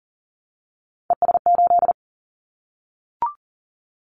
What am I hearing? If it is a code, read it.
Morse code, heard as a keyed tone, about 38 wpm, 700 Hz. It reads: ES8
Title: Callsign Prefixes for Top 50 DXCC - 40wpm